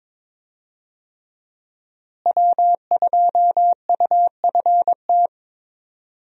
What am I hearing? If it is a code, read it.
Morse code, heard as a keyed tone, about 21 wpm, 700 Hz. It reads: W2UFT
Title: USA Call Signs - 5 Characters - Repeated 3x - 22wpm